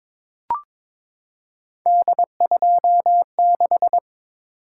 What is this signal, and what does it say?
Morse code, heard as a keyed tone, about 22 wpm, 700 Hz. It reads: D26
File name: practice / 22wpm / Three Letter-Number Combinations - 22wpm